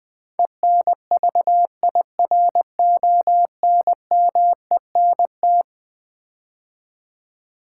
Morse code, heard as a keyed tone, about 20 wpm, 700 Hz. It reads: ENVIRONMENT
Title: Build A Word - 4 Letters At A Time - 11-Letter Words - 20wpm